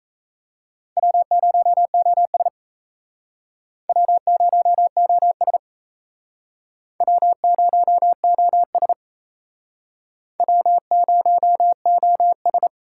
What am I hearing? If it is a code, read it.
Morse code, heard as a keyed tone, about 37 wpm, 700 Hz. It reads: W0OS W0OS W0OS W0OS